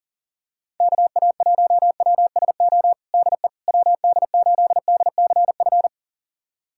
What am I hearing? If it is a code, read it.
Morse code, heard as a keyed tone, about 40 wpm, 690 Hz. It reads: KA1WSO DE WD8DCF
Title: USA Callsign DE Callsign - 6 Characters - 40wpm